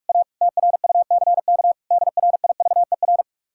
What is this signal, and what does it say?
Morse code, heard as a keyed tone, about 44 wpm, 700 Hz. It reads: A TRUCK DRIVER